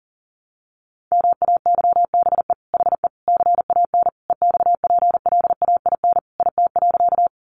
Morse code, heard as a keyed tone, about 40 wpm, 700 Hz. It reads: MAYBE HE CAN EXPLAIN IT.
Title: Sentences from Top 1100 Words 40wpm